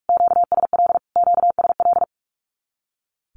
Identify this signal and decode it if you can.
Morse code, heard as a keyed tone, about 44 wpm, 700 Hz. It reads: QSL QSL
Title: Q-Codes Repeated 3x 45wpm